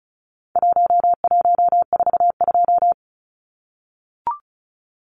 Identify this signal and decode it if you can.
Morse code, heard as a keyed tone, about 34 wpm, 700 Hz. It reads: 1142
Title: Numbers - Contest Style - 4-Digit 35wpm